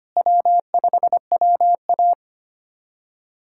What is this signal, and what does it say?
Morse code, heard as a keyed tone, about 25 wpm, 700 Hz. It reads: W5WA
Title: USA Callsign DE Callsign - 4 Characters - 25wpm